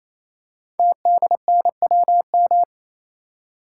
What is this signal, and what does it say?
Morse code, heard as a keyed tone, about 27 wpm, 700 Hz. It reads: TDNWM
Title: Code Groups - 5 Letters - 28wpm